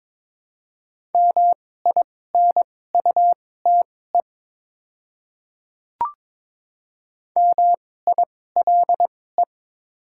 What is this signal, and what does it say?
Morse code, heard as a keyed tone, about 22 wpm, spaced as for 16 wpm, 700 Hz. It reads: MINUTE MILE
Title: 2x Character Spacing - Top 500 Words - 22wpm